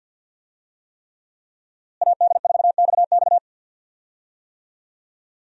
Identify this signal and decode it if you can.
Morse code, heard as a keyed tone, about 53 wpm, 700 Hz. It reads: AD4XX
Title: USA Call Signs - 5 Characters - Repeated 3x - 50wpm